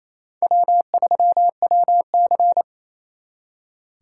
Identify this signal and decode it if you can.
Morse code, heard as a keyed tone, about 35 wpm, 700 Hz. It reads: W3WC